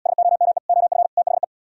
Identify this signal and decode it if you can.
Morse code, heard as a keyed tone, about 78 wpm, 700 Hz. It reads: SOME OF THE